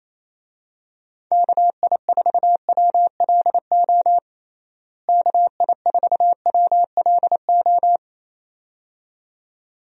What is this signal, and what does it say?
Morse code, heard as a keyed tone, about 28 wpm, 700 Hz. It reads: KI4WLO KI4WLO
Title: USA Call Signs - 6 Characters - Repeated 2x - 28wpm